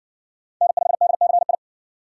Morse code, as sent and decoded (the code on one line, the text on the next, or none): -. ..... -.. -.-. ..
N5DCI